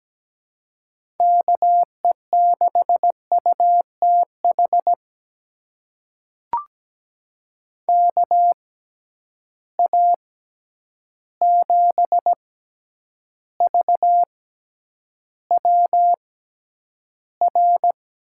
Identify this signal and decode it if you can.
Morse code, heard as a keyed tone, about 17 wpm, spaced as for 9 wpm, 700 Hz. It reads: KE6UTH KA7VWR
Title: USA Call Signs - 6 Characters - 6x Character Spacing - 17wpm